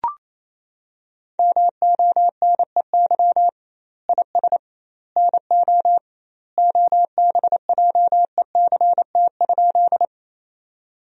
Morse code, heard as a keyed tone, about 28 wpm, 700 Hz. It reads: MONEY IS NO OBJECT?
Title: 2x Word-Spacing - Sentences from Top 500 Words - 28wpm